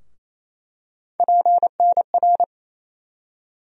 Morse code, heard as a keyed tone, about 28 wpm, 700 Hz. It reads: PNR
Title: USA Call Sign Postfixes - 3 Letters 28wpm